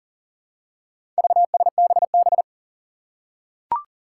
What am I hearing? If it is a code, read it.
Morse code, heard as a keyed tone, about 37 wpm, 700 Hz. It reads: VSBB
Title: Four Challenging Letter-Number Combinations 40wpm